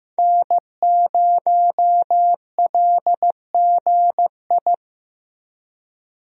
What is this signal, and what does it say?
Morse code, heard as a keyed tone, about 15 wpm, 700 Hz. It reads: N0LGI
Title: USA Call Signs - 5 Characters - Repeated 2x - 15wpm